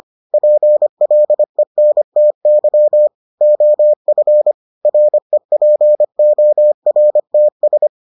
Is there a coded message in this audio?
Yes: Morse code from 0.3 to 7.9 s, heard as a keyed tone, about 25 wpm, 600 Hz. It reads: PLENTY OF REPORTS